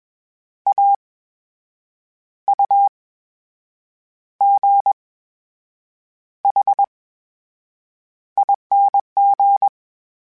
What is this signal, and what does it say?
Morse code, heard as a keyed tone, about 21 wpm, spaced as for 7 wpm, 800 Hz. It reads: AUGHING
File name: Special INGs-mp3s / Special INGs-mp3s / VerySlowToFaster Ing @20